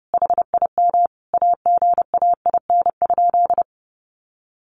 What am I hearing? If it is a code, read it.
Morse code, heard as a keyed tone, about 30 wpm, 700 Hz. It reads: HIM AGAIN?
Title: Sentences from Top 1400 Words 30wpm